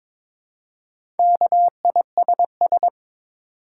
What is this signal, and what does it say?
Morse code, heard as a keyed tone, about 23 wpm, 700 Hz. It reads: KISS
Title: Four Letter Words - 22wpm